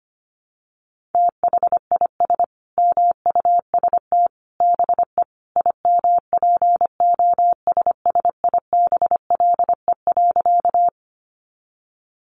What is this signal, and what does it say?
Morse code, heard as a keyed tone, about 25 wpm, 700 Hz. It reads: THIS MUST BE IMPOSSIBLE.
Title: Sentences from Top 1100 Words 25wpm